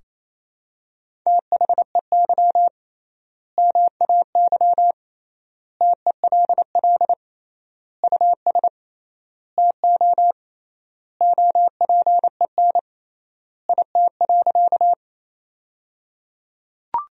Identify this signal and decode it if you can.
Morse code, heard as a keyed tone, about 28 wpm, 700 Hz. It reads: THEY MAY TELL US TO OPEN IT.
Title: 3x Word-Spacing - Sentences from Top 300 Words - 28wpm